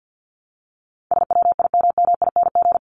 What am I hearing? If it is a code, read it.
Morse code, heard as a keyed tone, about 50 wpm, 700 Hz. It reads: SWIMMING
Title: Top 1501-1750 Words 50wpm